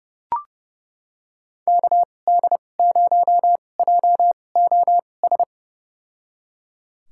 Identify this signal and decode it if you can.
Morse code, heard as a keyed tone, about 32 wpm, spaced as for 22 wpm, 700 Hz. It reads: KD0JOS
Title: USA Call Signs - 6 Characters - 2x Character Spacing - 30wpm